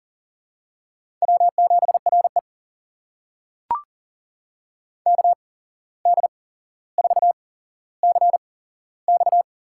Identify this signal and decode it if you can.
Morse code, heard as a keyed tone, about 39 wpm, spaced as for 16 wpm, 700 Hz. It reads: W7RE KD4CX
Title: USA Call Signs - 8x Character Spacing - 40wpm